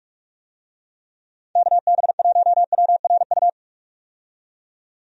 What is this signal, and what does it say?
Morse code, heard as a keyed tone, about 46 wpm, 700 Hz. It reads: KB1WRU
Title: USA Call Signs - 6 Characters - 45wpm